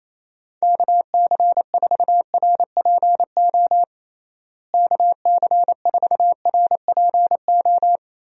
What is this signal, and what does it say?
Morse code, heard as a keyed tone, about 28 wpm, 700 Hz. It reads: KC4RPO KC4RPO
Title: USA Call Signs - 6 Characters - Repeated 3x - 28wpm